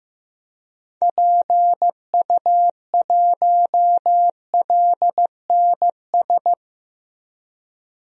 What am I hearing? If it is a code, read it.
Morse code, heard as a keyed tone, about 15 wpm, 700 Hz. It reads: PU1LNS
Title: DXCC-100 Call Signs - 6 Characters - Repeated 3x - 15wpm